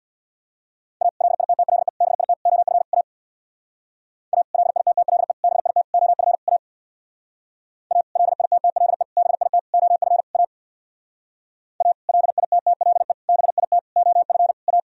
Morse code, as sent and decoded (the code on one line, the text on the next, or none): .- / .-.. .. - - .-.. / -... .. - / --- ..-. / .- / .- / .-.. .. - - .-.. / -... .. - / --- ..-. / .- / .- / .-.. .. - - .-.. / -... .. - / --- ..-. / .- / .- / .-.. .. - - .-.. / -... .. - / --- ..-. / .-
A LITTLE BIT OF A A LITTLE BIT OF A A LITTLE BIT OF A A LITTLE BIT OF A